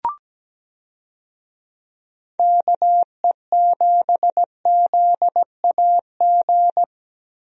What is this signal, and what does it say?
Morse code, heard as a keyed tone, about 17 wpm, 700 Hz. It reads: KE7ZAG
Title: USA Call Signs - 6 Characters - Repeated 3x - 17wpm